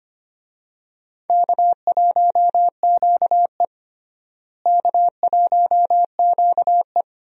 Morse code, heard as a keyed tone, about 25 wpm, 700 Hz. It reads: K1QE K1QE
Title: USA Call Signs - 4 Characters - Repeated 2x - 25wpm